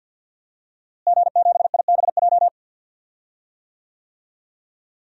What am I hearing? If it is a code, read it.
Morse code, heard as a keyed tone, about 50 wpm, 700 Hz. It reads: G7IBJ
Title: DXCC-100 Call Signs - 5 Characters - Repeated 3x - 50wpm